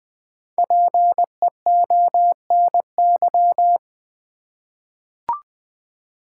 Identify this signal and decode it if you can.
Morse code, heard as a keyed tone, about 20 wpm, 690 Hz. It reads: PEONY
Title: Difficult Words with Hints - 20wpm